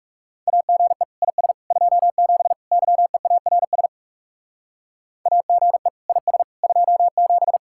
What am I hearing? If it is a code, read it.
Morse code, heard as a keyed tone, about 37 wpm, 700 Hz. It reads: AGE IS 27 YEARS AGE IS 27
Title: Speed-Racing Common QSO Phrases Encore - 30wpm